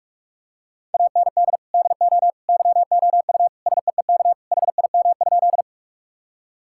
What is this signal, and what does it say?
Morse code, heard as a keyed tone, about 41 wpm, 700 Hz. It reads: AND DO YOU SEEK HIM?
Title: Sentences from Top 1600 Words 45wpm